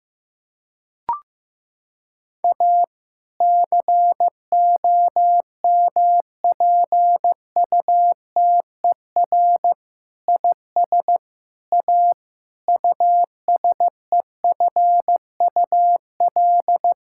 Morse code, heard as a keyed tone, about 15 wpm, 700 Hz. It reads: A COMPUTER IS A USEFUL
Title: Sentences from Top 1000 Words - Review Part 6 - 15wpm